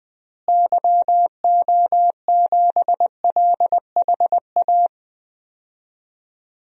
Morse code, heard as a keyed tone, about 21 wpm, 700 Hz. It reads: YO7LHA
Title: DXCC-100 Call Signs - 6 Characters - Repeated 3x - 20wpm